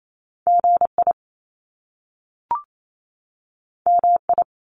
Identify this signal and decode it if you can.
Morse code, heard as a keyed tone, about 28 wpm, 700 Hz. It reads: GI MI
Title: Top Two Letter Combinations in Words 28wpm